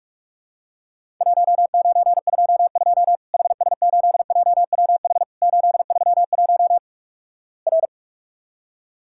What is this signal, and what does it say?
Morse code, heard as a keyed tone, about 44 wpm, 700 Hz. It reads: 1922 HS8JWH 831 R
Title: Worked All Europe DX Contest - QTC Exchange - 45wpm